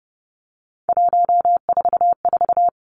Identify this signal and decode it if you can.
Morse code, heard as a keyed tone, about 30 wpm, 700 Hz. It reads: 144
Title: Three Challenging Letter-Number Combinations 30wpm